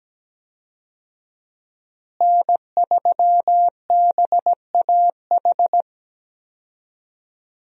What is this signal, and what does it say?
Morse code, heard as a keyed tone, about 17 wpm, 700 Hz. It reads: N3BAH
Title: USA Call Signs - 5 Characters - Repeated 2x - 17wpm